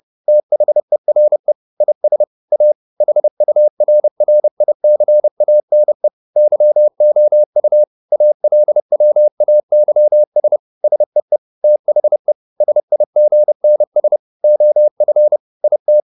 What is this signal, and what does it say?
Morse code, heard as a keyed tone, about 30 wpm, 600 Hz. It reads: THERE IS A HURRICANE YOU ALWAYS SEE THE SIGNS OF IT
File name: the old man and the sea / the old man and the sea 1 30wpm